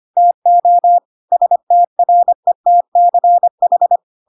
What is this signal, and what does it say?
Morse code, heard as a keyed tone, about 25 wpm, 700 Hz. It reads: TO STRETCH
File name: Pinocchio / Pinocchio ch3 25 25 ik1hge